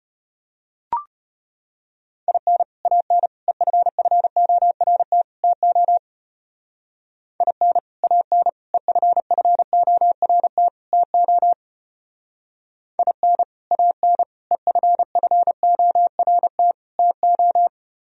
Morse code, heard as a keyed tone, about 31 wpm, 700 Hz. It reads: IN AN EFFORT TO IN AN EFFORT TO IN AN EFFORT TO